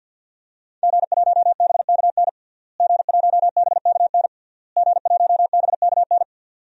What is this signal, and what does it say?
Morse code, heard as a keyed tone, about 52 wpm, 700 Hz. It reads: G1BKN G1BKN G1BKN
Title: DXCC-100 Call Signs - 5 Characters - Repeated 3x - 50wpm